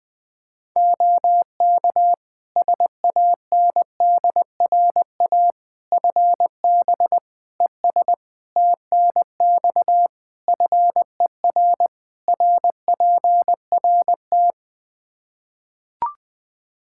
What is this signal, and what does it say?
Morse code, heard as a keyed tone, about 20 wpm, 700 Hz. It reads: OK SANDRA FB ES TNX FER RPRT
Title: Practice QSO - Exchange 2 - Info - 20wpm